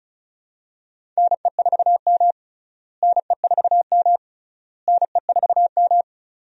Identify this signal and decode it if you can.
Morse code, heard as a keyed tone, about 34 wpm, 700 Hz. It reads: NE4M NE4M NE4M
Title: USA Call Signs - 4 Characters - Repeated 3x - 35wpm